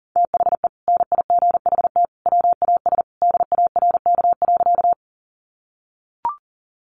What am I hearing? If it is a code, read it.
Morse code, heard as a keyed tone, about 40 wpm, 700 Hz. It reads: THE NIGHT WAS DARK.
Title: Sentences from Top 500 Words 40wpm